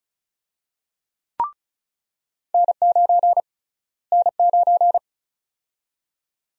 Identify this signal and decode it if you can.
Morse code, heard as a keyed tone, about 34 wpm, 700 Hz. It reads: N9 N9
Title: USA Call Sign Prefixes - Repeated 2x - 35wpm